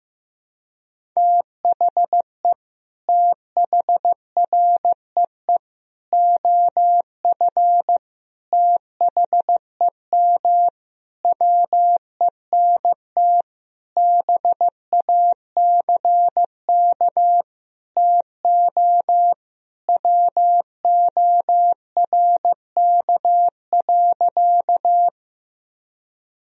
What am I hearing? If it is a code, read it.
Morse code, heard as a keyed tone, about 15 wpm, 700 Hz. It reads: THE THREE OF THEM WENT BACK TO WORK.
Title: Sentences from Top 1000 Words - Review Part 1 - 15wpm